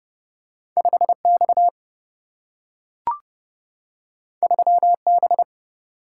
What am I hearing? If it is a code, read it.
Morse code, heard as a keyed tone, about 30 wpm, 700 Hz. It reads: 5X 3B
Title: Single-Letter Single-Number 30wpm